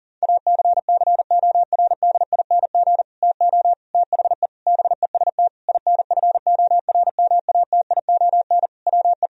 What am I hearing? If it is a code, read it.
Morse code, heard as a keyed tone, about 40 wpm, 700 Hz. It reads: ACCORDING TO THE BEST INFORMATION WE
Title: Sentences from Top 1000 Words - Review Part 4 - 40wpm